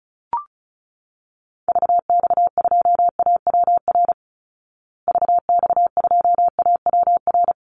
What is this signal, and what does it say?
Morse code, heard as a keyed tone, about 34 wpm, 700 Hz. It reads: VX2AWR VX2AWR
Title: Call Signs Repeated 2x - 35wpm